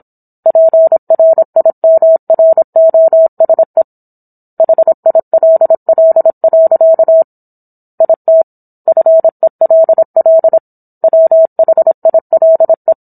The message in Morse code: PRIMROSE HILL. IT FELL WHILE

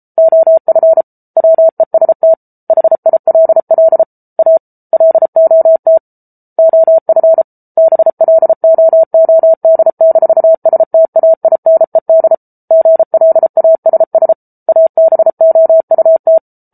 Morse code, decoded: OF WEST HILL A LOT OF BLOOD-STAINED GLASS ABOUT

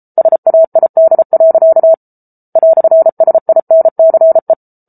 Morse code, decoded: SAID. "SINCE

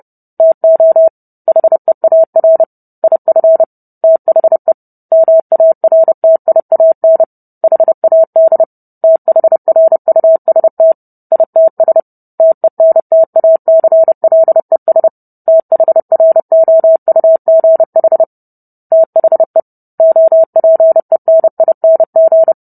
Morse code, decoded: TO HEAR IF THE MARTIAN HAD THRUST ITS TENTACLES THROUGH THE OPENING

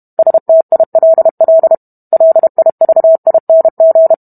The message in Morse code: STILL LIVING

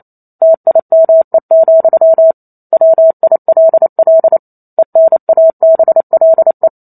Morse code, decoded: TIME, WILL ENABLE

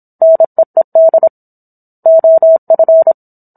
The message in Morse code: NEED OF

